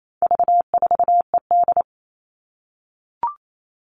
Morse code, heard as a keyed tone, about 28 wpm, 700 Hz. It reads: V4ED
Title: All Cut Numbers - 4-Digit 28wpm